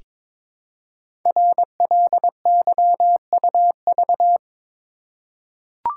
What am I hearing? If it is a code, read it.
Morse code, heard as a keyed tone, about 22 wpm, 700 Hz. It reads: RLYUV
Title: Code Groups - 5 Letters - 22wpm